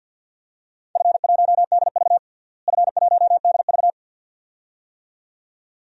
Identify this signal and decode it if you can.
Morse code, heard as a keyed tone, about 48 wpm, 700 Hz. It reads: F1DV F1DV
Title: DXCC-100 Call Signs - 4 Characters - Repeated 2x - 50wpm